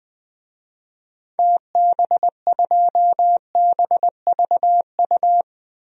Morse code, heard as a keyed tone, about 20 wpm, 700 Hz. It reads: TB2BVU
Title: DXCC-100 Call Signs - 6 Characters - 20wpm